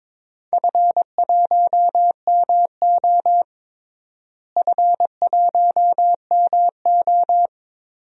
Morse code, heard as a keyed tone, about 22 wpm, 700 Hz. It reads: F1MO F1MO
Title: DXCC-100 Call Signs - 4 Characters - Repeated 3x - 22wpm